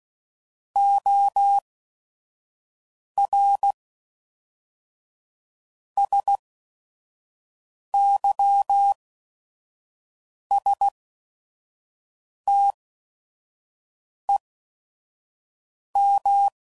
Morse code, heard as a keyed tone, about 16 wpm, spaced as for 5 wpm, 800 Hz. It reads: OR SYSTEM
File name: CW Code Groups @5WPM